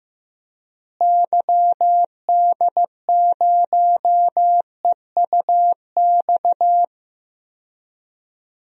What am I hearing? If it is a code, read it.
Morse code, heard as a keyed tone, about 15 wpm, 700 Hz. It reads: YD0EUX
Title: DXCC-100 Call Signs - 6 Characters - Repeated 3x - 15wpm